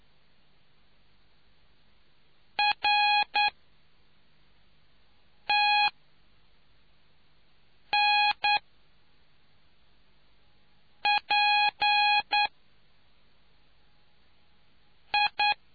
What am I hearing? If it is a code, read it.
Morse code, heard as a keyed tone, about 9 wpm, spaced as for 3 wpm, 830 Hz. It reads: RTNPI